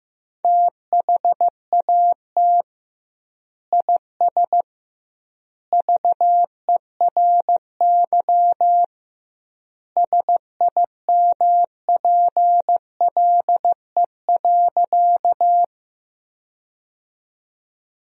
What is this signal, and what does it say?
Morse code, heard as a keyed tone, about 15 wpm, 700 Hz. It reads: THAT IS VERY SIMPLE.
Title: 2x Word-Spacing - Sentences from Top 400 Words - 15wpm